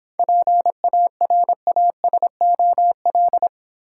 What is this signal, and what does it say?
Morse code, heard as a keyed tone, about 26 wpm, 700 Hz. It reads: PARASOL